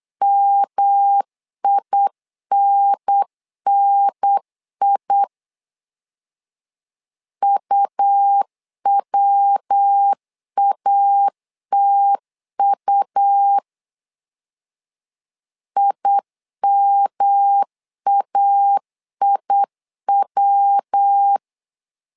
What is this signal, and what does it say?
Morse code, heard as a keyed tone, about 8 wpm, 800 Hz. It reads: MINNI UWATU IMAIW
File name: LEZIONI / 3 MWRO / 3 MW 8wpm 2501